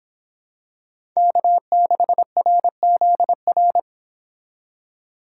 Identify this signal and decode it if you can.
Morse code, heard as a keyed tone, about 25 wpm, 700 Hz. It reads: K6RZR